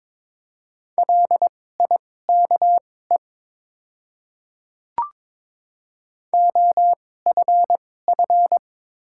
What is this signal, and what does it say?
Morse code, heard as a keyed tone, about 21 wpm, spaced as for 16 wpm, 690 Hz. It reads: LIKE OFF
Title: 2x Character Spacing - Top 200 Words - 22wpm